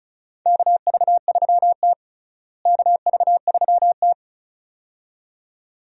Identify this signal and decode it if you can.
Morse code, heard as a keyed tone, about 36 wpm, 700 Hz. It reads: KV3T KV3T
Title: USA Call Signs - 4 Characters - Repeated 2x - 35wpm